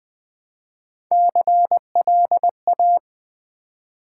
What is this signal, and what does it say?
Morse code, heard as a keyed tone, about 20 wpm, 700 Hz. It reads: CLA
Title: Build A Word - 1 Letter At A Time - 5-Letter Words - 20wpm